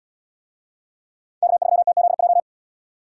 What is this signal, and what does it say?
Morse code, heard as a keyed tone, about 70 wpm, 700 Hz. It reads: K3TZJ